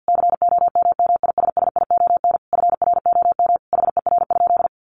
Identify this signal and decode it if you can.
Morse code, heard as a keyed tone, about 50 wpm, 700 Hz. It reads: COMMISSION FROM HER?
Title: Sentences from Top 1400 Words 50wpm